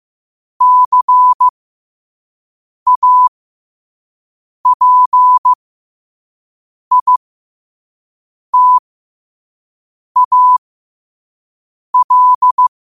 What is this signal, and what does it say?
Morse code, heard as a keyed tone, about 15 wpm, spaced as for 5 wpm, 1000 Hz. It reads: CAPITAL
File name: qotdfiles / 5wpm